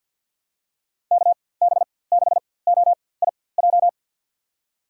Morse code, heard as a keyed tone, about 51 wpm, spaced as for 23 wpm, 700 Hz. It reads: KB6OIJ
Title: USA Call Signs - 6 Characters - 4x Character Spacing - 50wpm